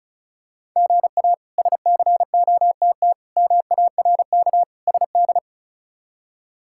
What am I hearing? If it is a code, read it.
Morse code, heard as a keyed tone, about 36 wpm, 700 Hz. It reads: GA SCOTT MARK SD
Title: K1USN SST QSO Practice - 35wpm